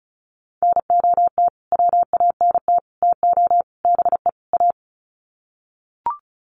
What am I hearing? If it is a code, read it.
Morse code, heard as a keyed tone, about 34 wpm, 700 Hz. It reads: NOT WANT TO BE A
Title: Sets of 5 Words Encore 35wpm